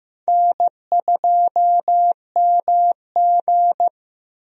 Morse code, heard as a keyed tone, about 15 wpm, 700 Hz. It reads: N2MG
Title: USA Call Signs - 4 Characters - Repeated 3x - 15wpm